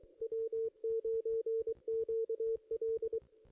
Morse code, heard as a keyed tone, about 22 wpm, 440 Hz. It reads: W9QL
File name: contests / ss-cw-k5tr-op-n5rz -1600